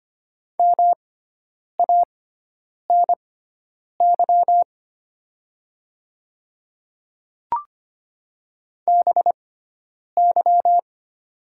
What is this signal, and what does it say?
Morse code, heard as a keyed tone, about 25 wpm, spaced as for 9 wpm, 690 Hz. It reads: MANY BY
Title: 6x Character Spacing - Top 100 Words - 25wpm